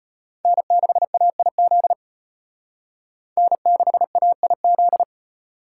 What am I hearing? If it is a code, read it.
Morse code, heard as a keyed tone, about 32 wpm, 700 Hz. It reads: N6AIZ N6AIZ